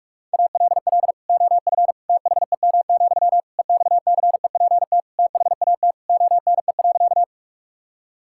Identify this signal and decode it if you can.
Morse code, heard as a keyed tone, about 41 wpm, 700 Hz. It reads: ALL OF THEM, EXCEPT THAT ONE.
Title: Sentences from Top 1600 Words 45wpm